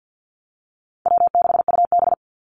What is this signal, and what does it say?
Morse code, heard as a keyed tone, about 50 wpm, 700 Hz. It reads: W6UB
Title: NA SOTA Activator Callsigns 50wpm